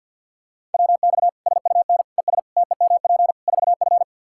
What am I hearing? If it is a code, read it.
Morse code, heard as a keyed tone, about 47 wpm, 700 Hz. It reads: WX SUN ES TEMP 4F
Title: Practice QSO - Exchange 2 - Info - 50wpm